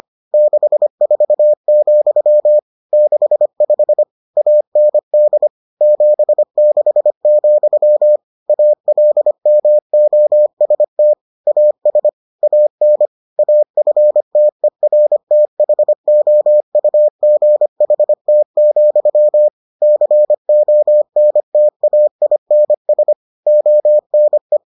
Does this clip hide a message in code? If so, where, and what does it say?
Yes: Morse code from 0.3 to 24.6 s, heard as a keyed tone, about 25 wpm, 600 Hz. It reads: 64, 65 AND 76, ALMOST AS AN AFTERTHOUGHT, CONTAINS ONE